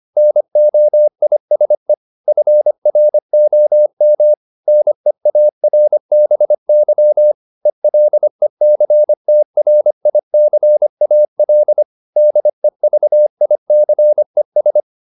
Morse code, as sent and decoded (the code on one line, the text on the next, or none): -. --- .. ... / ..-. .-. --- -- / -. .- .-. -... -.-- / . .-.. -.-. - .-. .. -.-. .- .-.. / -.. ...- .. -.-. ...
NOISE FROM NEARBY ELECTRICAL DEVICES